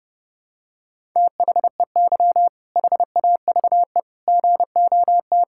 Morse code, heard as a keyed tone, about 32 wpm, 700 Hz. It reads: THEY HAVE GOT
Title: Sentences from Top 1000 Words - Review Part 6 - 30wpm